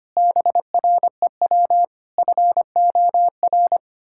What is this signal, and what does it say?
Morse code, heard as a keyed tone, about 25 wpm, 700 Hz. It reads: BREW FOR
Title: Pangram Challenge v2 - 25wpm